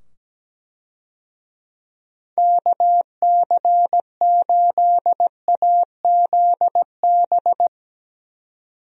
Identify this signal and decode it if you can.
Morse code, heard as a keyed tone, about 17 wpm, 700 Hz. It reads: KC8AZB
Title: USA Call Signs - 6 Characters - 17wpm